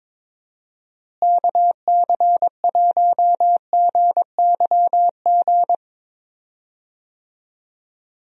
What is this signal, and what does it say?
Morse code, heard as a keyed tone, about 22 wpm, 700 Hz. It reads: KC1GYG
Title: USA Call Signs - 6 Characters - Repeated 2x - 22wpm